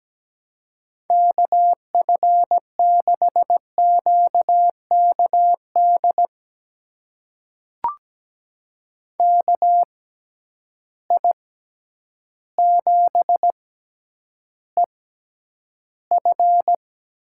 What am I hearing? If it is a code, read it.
Morse code, heard as a keyed tone, about 17 wpm, spaced as for 9 wpm, 700 Hz. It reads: KF6QKD KI7EF
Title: USA Call Signs - 6 Characters - 6x Character Spacing - 17wpm